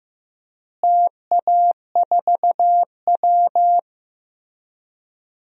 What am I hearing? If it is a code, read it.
Morse code, heard as a keyed tone, about 15 wpm, 700 Hz. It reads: TA4W
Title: DXCC-100 Call Signs - 4 Characters - Repeated 3x - 15wpm